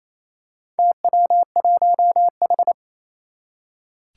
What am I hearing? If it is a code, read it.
Morse code, heard as a keyed tone, about 27 wpm, 700 Hz. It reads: TW1H
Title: Four Challenging Letter-Number Combinations 28wpm